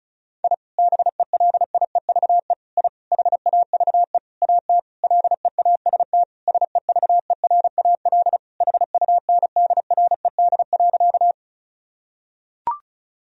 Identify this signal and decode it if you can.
Morse code, heard as a keyed tone, about 36 wpm, 700 Hz. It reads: I BELIEVE I HAVE AT LEAST SEVERAL HUNDRED.
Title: Sentences from Top 1000 Words - Review Part 2 - 35wpm